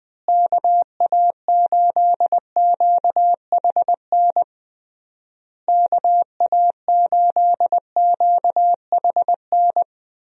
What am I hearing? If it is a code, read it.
Morse code, heard as a keyed tone, about 21 wpm, 700 Hz. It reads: KA8QHN KA8QHN
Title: USA Call Signs - 6 Characters - Repeated 3x - 20wpm